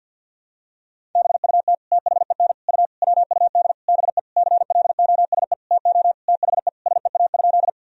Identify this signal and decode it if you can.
Morse code, heard as a keyed tone, about 52 wpm, 700 Hz. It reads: BUT THEN U WUD BE CLOSE TO THE SEA?
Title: Sentences from Top 300 Words and Abbreviations 50wpm